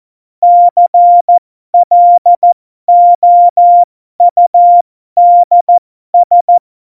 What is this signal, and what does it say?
Morse code, heard as a keyed tone, about 14 wpm, spaced as for 12 wpm, 700 Hz. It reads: CLOUDS
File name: Morsekurs / Witze